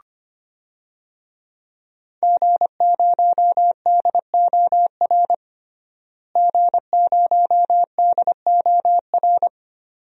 Morse code, heard as a keyed tone, about 25 wpm, 700 Hz. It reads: G0DOR G0DOR
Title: DXCC-100 Call Signs - 5 Characters - Repeated 2x - 25wpm